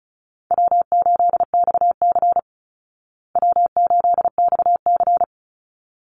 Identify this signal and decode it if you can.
Morse code, heard as a keyed tone, about 34 wpm, 700 Hz. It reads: W8XC W8XC
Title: Easy Call Signs Repeated 2x - 35wpm